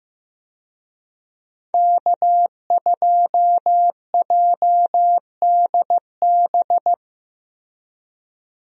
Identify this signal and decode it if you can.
Morse code, heard as a keyed tone, about 16 wpm, 700 Hz. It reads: K2JDB
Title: USA Call Signs - 5 Characters - Repeated 3x - 15wpm